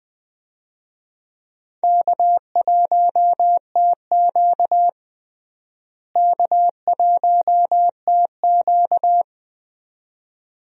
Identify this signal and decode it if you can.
Morse code, heard as a keyed tone, about 20 wpm, 700 Hz. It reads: K1TQ K1TQ
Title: USA Call Signs - 4 Characters - Repeated 2x - 20wpm